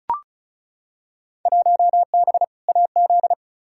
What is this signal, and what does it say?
Morse code, heard as a keyed tone, about 34 wpm, 700 Hz. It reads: 1B AZ
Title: Field Day Exchanges Encore - 35wpm